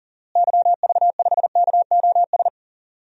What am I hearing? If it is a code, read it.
Morse code, heard as a keyed tone, about 38 wpm, 700 Hz. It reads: YV5KOS
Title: DXCC-100 Call Signs - 6 Characters - Repeated 2x - 40wpm